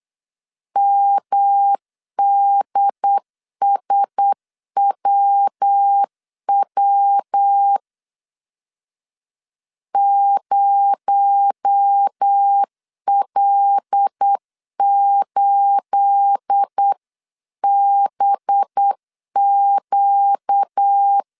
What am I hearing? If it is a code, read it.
Morse code, heard as a keyed tone, about 8 wpm, 800 Hz. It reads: MDSWW 0L8BQ